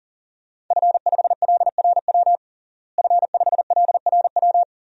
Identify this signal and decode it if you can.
Morse code, heard as a keyed tone, about 40 wpm, 700 Hz. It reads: F5LRW F5LRW
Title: DXCC-100 Call Signs - 5 Characters - Repeated 2x - 40wpm